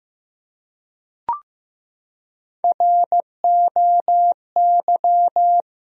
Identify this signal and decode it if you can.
Morse code, heard as a keyed tone, about 15 wpm, 700 Hz. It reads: ROY
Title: USA Common Names - Max 5 Letters - 15wpm